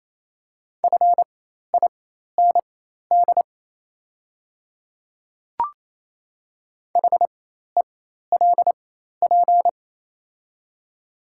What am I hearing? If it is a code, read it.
Morse code, heard as a keyed tone, about 28 wpm, spaced as for 13 wpm, 690 Hz. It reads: FIND HELP
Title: 4x Character Spacing - Top 200 Words - 28wpm